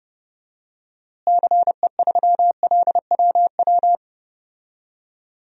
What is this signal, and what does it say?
Morse code, heard as a keyed tone, about 31 wpm, 700 Hz. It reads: CE3LWW
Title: DXCC-100 Call Signs - 6 Characters - Repeated 3x - 30wpm